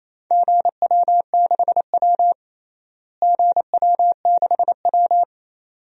Morse code, heard as a keyed tone, about 28 wpm, 700 Hz. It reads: GW6W GW6W
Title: DXCC-100 Call Signs - 4 Characters - Repeated 2x - 28wpm